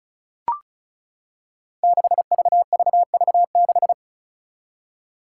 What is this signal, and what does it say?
Morse code, heard as a keyed tone, about 34 wpm, 700 Hz. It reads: 6VVV6
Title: Five Challenging Letter-Number Combinations 35wpm